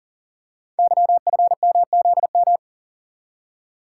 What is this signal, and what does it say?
Morse code, heard as a keyed tone, about 39 wpm, 700 Hz. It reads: YFMZM
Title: Code Groups - 5 Letters - 40wpm